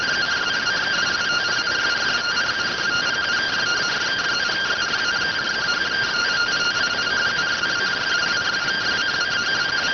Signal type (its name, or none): two-tone teleprinter signal